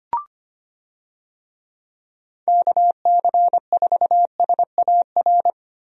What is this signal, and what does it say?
Morse code, heard as a keyed tone, about 25 wpm, 700 Hz. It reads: KC4SAR
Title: USA Call Signs - 6 Characters - Repeated 2x - 25wpm